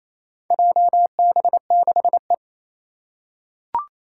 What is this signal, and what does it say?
Morse code, heard as a keyed tone, about 27 wpm, 700 Hz. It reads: JB6E
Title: Four Challenging Letter-Number Combinations 28wpm